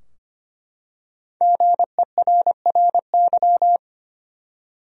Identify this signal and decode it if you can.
Morse code, heard as a keyed tone, about 25 wpm, 700 Hz. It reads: GERRY
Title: USA Common Names - 25wpm